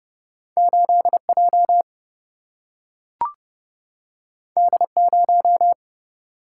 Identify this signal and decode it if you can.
Morse code, heard as a keyed tone, about 29 wpm, 700 Hz. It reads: 8 J D 0
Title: Single-Letter Single-Number 30wpm